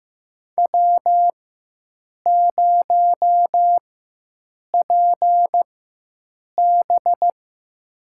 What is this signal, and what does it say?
Morse code, heard as a keyed tone, about 15 wpm, spaced as for 7 wpm, 700 Hz. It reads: W0PB
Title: USA Call Signs - 4x Character Spacing - 15wpm